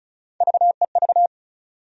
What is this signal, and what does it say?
Morse code, heard as a keyed tone, about 36 wpm, 700 Hz. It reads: VEV
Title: USA Call Sign Postfixes - 3 Letters 35wpm